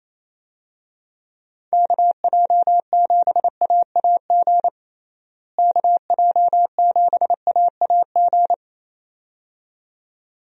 Morse code, heard as a keyed tone, about 28 wpm, 700 Hz. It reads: KJ7AAG KJ7AAG
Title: USA Call Signs - 6 Characters - Repeated 2x - 28wpm